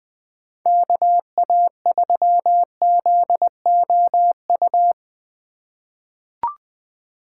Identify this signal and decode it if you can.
Morse code, heard as a keyed tone, about 21 wpm, 700 Hz. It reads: KA3ZOU
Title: USA Call Signs - 6 Characters - 4x Character Spacing - 20wpm